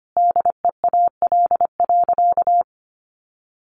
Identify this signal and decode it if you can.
Morse code, heard as a keyed tone, about 25 wpm, 700 Hz. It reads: DEAL.
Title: Sentences from Top 1200 Words 25wpm